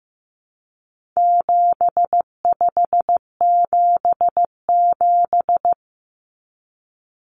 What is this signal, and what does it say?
Morse code, heard as a keyed tone, about 15 wpm, 700 Hz. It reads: 7577
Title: Common Cut Numbers - 4-Digit 15wpm